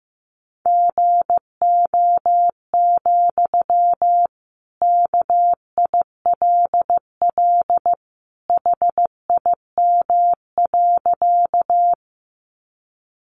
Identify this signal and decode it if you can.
Morse code, heard as a keyed tone, about 14 wpm, 700 Hz. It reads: GO, KILL HIM.
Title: Sentences from Top 1400 Words 15wpm